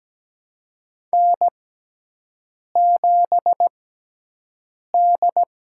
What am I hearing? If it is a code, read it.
Morse code, heard as a keyed tone, about 17 wpm, spaced as for 6 wpm, 700 Hz. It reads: N7D
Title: USA Call Signs - 6x Character Spacing - 17wpm